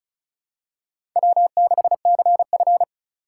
Morse code, heard as a keyed tone, about 36 wpm, 700 Hz. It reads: W6CF
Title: USA Call Signs - 4 Characters - 4x Character Spacing - 35wpm